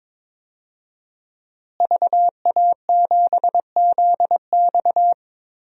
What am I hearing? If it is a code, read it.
Morse code, heard as a keyed tone, about 22 wpm, 700 Hz. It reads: VA7ZX
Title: DXCC-100 Call Signs - 5 Characters - Repeated 2x - 22wpm